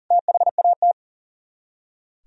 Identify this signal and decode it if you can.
Morse code, heard as a keyed tone, about 50 wpm, 700 Hz. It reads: THAT